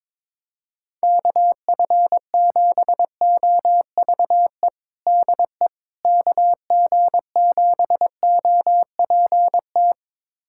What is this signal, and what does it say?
Morse code, heard as a keyed tone, about 22 wpm, 700 Hz. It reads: KF7OVE DE KG7OPT
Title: USA Callsign DE Callsign - 6 Characters - 22wpm